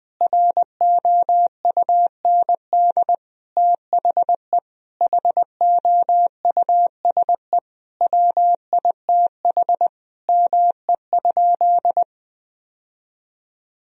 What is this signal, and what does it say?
Morse code, heard as a keyed tone, about 20 wpm, 700 Hz. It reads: ROUND THE HOUSE WITH ME?
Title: Sentences from Top 1000 Words - Review Part 1 - 20wpm